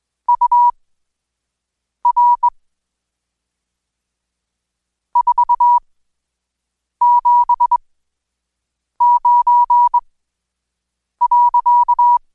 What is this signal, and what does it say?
Morse code, heard as a keyed tone, about 20 wpm, spaced as for 6 wpm, 970 Hz. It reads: UR 479.
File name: Code Quick/AudioDownld2/Seg109-Day14-DayDrill 14